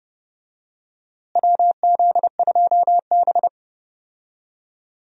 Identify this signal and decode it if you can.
Morse code, heard as a keyed tone, about 30 wpm, 700 Hz. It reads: WZ2B
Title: USA Call Signs - 4 Characters - Repeated 2x - 30wpm